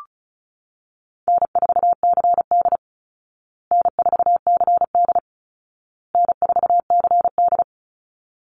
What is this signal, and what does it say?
Morse code, heard as a keyed tone, about 34 wpm, 700 Hz. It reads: N4CD N4CD N4CD
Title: Easy Call Signs Repeated 3x - 35wpm